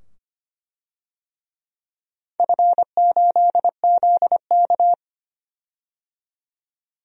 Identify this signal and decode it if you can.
Morse code, heard as a keyed tone, about 25 wpm, 700 Hz. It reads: F8ZK
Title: DXCC-100 Call Signs - 4 Characters - 25wpm